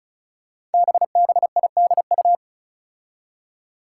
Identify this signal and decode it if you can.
Morse code, heard as a keyed tone, about 36 wpm, 700 Hz. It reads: BBIDU
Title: Five Challenging Letter-Number Combinations 35wpm